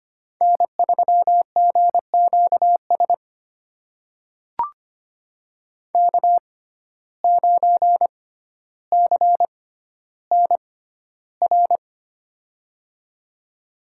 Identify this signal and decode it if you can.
Morse code, heard as a keyed tone, about 25 wpm, spaced as for 13 wpm, 700 Hz. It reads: N3GQS K9CNR
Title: USA Call Signs - 5 Characters - 6x Character Spacing - 25wpm